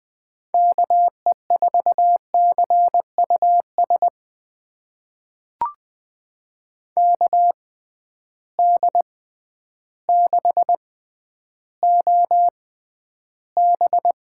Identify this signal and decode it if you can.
Morse code, heard as a keyed tone, about 20 wpm, spaced as for 11 wpm, 700 Hz. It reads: KE4CUS KD6OB
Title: USA Call Signs - 6 Characters - 6x Character Spacing - 20wpm